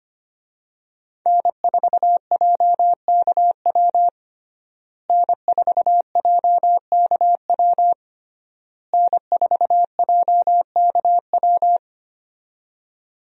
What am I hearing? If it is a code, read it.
Morse code, heard as a keyed tone, about 25 wpm, 700 Hz. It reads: N4JKW N4JKW N4JKW
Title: USA Call Signs - 5 Characters - Repeated 3x - 25wpm